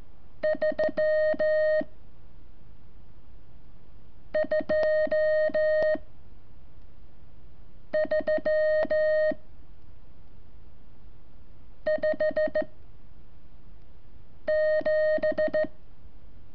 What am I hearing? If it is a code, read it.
Morse code, heard as a keyed tone, about 14 wpm, 620 Hz. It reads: 3 2 3 5 7
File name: curso / ejercicio 28